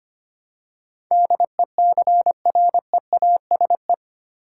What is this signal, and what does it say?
Morse code, heard as a keyed tone, about 25 wpm, 700 Hz. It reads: DECREASE
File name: Eight Letter Words - 25wpm